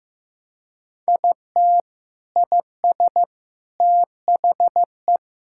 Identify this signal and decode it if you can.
Morse code, heard as a keyed tone, about 15 wpm, 700 Hz. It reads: IT IS THE
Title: Sentences from Top 1600 Words 15wpm